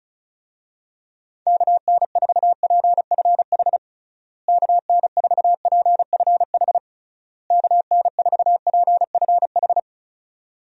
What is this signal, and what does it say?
Morse code, heard as a keyed tone, about 34 wpm, 700 Hz. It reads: KN4PFH KN4PFH KN4PFH
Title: USA Call Signs - 6 Characters - Repeated 3x - 35wpm